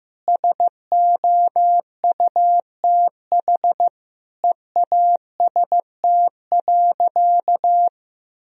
Morse code, heard as a keyed tone, about 15 wpm, 700 Hz. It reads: SOUTH EAST.
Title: Sentences from Top 1000 Words - Review Part 3 - 15wpm